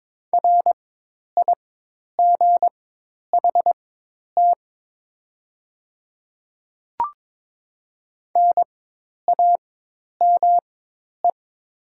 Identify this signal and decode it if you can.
Morse code, heard as a keyed tone, about 22 wpm, spaced as for 10 wpm, 700 Hz. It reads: RIGHT NAME
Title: 4x Character Spacing - Top 200 Words - 22wpm